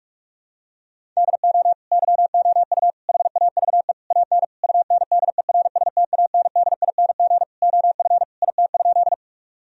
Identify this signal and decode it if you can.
Morse code, heard as a keyed tone, about 46 wpm, 700 Hz. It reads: DO YOU HAVE AN UNDERSTANDING OF IT?
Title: Sentences from Top 1000 Words - Review Part 3 - 45wpm